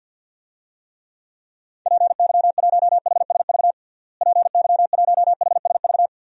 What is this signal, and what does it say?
Morse code, heard as a keyed tone, about 48 wpm, 700 Hz. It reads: PY1HSV PY1HSV
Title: DXCC-100 Call Signs - 6 Characters - Repeated 2x - 50wpm